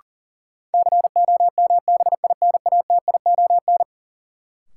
Morse code, heard as a keyed tone, about 40 wpm, 700 Hz. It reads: COMBINATION
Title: Sentences from Top 1000 Words - Review Part 6 - New Word Drill - 40wpm